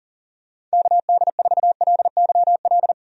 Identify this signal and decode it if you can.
Morse code, heard as a keyed tone, about 37 wpm, 700 Hz. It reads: KD4LYL
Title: USA Call Signs - 6 Characters - 40wpm